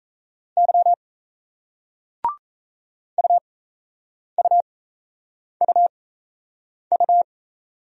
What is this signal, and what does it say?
Morse code, heard as a keyed tone, about 38 wpm, spaced as for 8 wpm, 700 Hz. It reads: Y UUUU